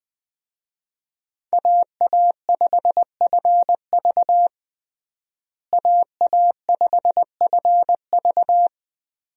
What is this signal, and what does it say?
Morse code, heard as a keyed tone, about 21 wpm, 700 Hz. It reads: AA5FV AA5FV
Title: USA Call Signs - 5 Characters - Repeated 2x - 20wpm